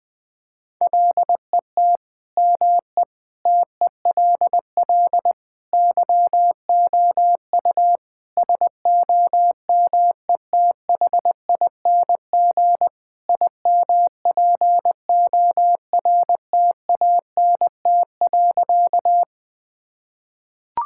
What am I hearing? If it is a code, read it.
Morse code, heard as a keyed tone, about 21 wpm, 700 Hz. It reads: LET ME TELL YOU SOMETHING IMPORTANT.
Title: Sentences with 5-Word Phrases - Spoken Beginning - 20wpm